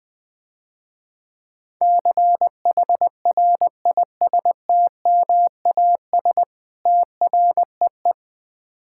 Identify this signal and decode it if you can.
Morse code, heard as a keyed tone, about 20 wpm, 700 Hz. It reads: CHRISTMAS TREE
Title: Easy Words - Speed-Racing with Hints - 20wpm